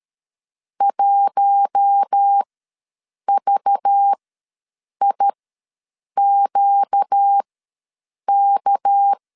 Morse code, heard as a keyed tone, about 13 wpm, spaced as for 7 wpm, 800 Hz. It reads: 1VIQK
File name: LEZIONI / 6 PKX0 / 6 PK ws 12wpm 3116